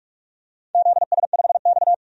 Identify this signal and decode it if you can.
Morse code, heard as a keyed tone, about 46 wpm, 700 Hz. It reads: ZS5X
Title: DXCC-100 Call Signs - 4 Characters - 45wpm